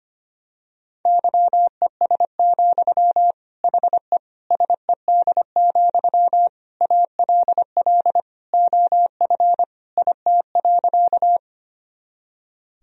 Morse code, heard as a keyed tone, about 25 wpm, 700 Hz. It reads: YES, HE SED, ALL OF IT.
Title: Sentences from Top 500 Words and Abbreviations 25wpm